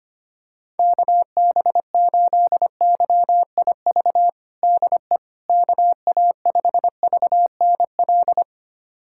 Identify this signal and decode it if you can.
Morse code, heard as a keyed tone, about 25 wpm, 700 Hz. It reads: KB8YIV DE KA5VNL
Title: USA Callsign DE Callsign - 6 Characters - 25wpm